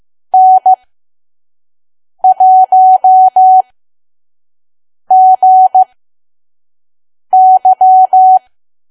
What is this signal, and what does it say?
Morse code, heard as a keyed tone, about 15 wpm, spaced as for 5 wpm, 750 Hz. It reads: N1GY
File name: Archive / 5 WPM files / PM